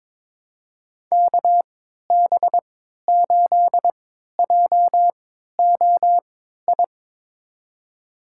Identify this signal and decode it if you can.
Morse code, heard as a keyed tone, about 21 wpm, spaced as for 12 wpm, 700 Hz. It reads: KB8JOI
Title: USA Call Signs - 6 Characters - 3x Character Spacing - 22wpm